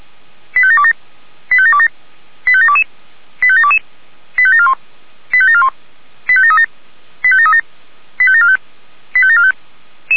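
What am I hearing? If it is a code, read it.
Five-tone selective call (ZVEI), heard as quick runs of five tones, tones of about 70 ms. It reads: 85727 85727 85722 85722 85741 85741 85737 85737 85745 85745
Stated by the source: ZVEI five-tone call received on SDR.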